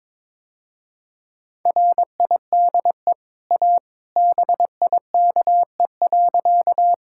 Morse code, heard as a keyed tone, about 22 wpm, 700 Hz. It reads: RIDE A BIKE.
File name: practice / 22wpm / Easy Sentences with Hints - 22wpm